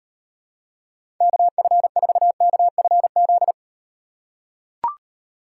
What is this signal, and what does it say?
Morse code, heard as a keyed tone, about 36 wpm, 700 Hz. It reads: KF4KFZ